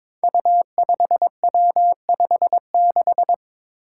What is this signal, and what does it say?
Morse code, heard as a keyed tone, about 22 wpm, 700 Hz. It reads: U5W56
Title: Five Challenging Letter-Number Combinations 22wpm